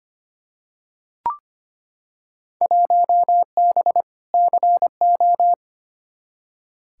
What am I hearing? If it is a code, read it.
Morse code, heard as a keyed tone, about 25 wpm, 700 Hz. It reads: 1B CO
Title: Field Day Exchanges - 25wpm